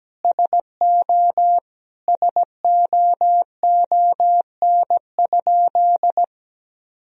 Morse code, heard as a keyed tone, about 17 wpm, 700 Hz. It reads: SO SOON?
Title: Sentences from Top 400 Words and Abbreviations 17wpm